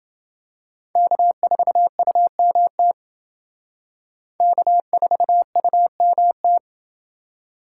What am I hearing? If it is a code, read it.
Morse code, heard as a keyed tone, about 28 wpm, 700 Hz. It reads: K4UMT K4UMT